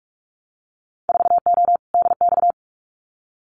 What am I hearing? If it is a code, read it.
Morse code, heard as a keyed tone, about 46 wpm, 700 Hz. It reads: 4O DX
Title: Winter Field Day Exchanges Encore 45wpm